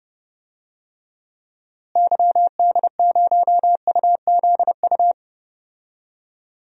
Morse code, heard as a keyed tone, about 32 wpm, 700 Hz. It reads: YD0UZU
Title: DXCC-100 Call Signs - 6 Characters - Repeated 3x - 30wpm